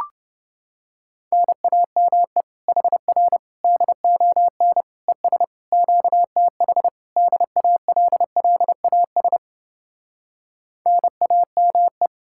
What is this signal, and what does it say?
Morse code, heard as a keyed tone, about 30 wpm, 700 Hz. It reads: NAME HR DON ES QTH DALLAS NAME